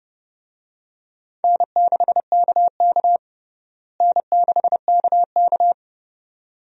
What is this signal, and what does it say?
Morse code, heard as a keyed tone, about 30 wpm, 700 Hz. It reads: N6KK N6KK
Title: USA Call Signs - 4 Characters - Repeated 2x - 30wpm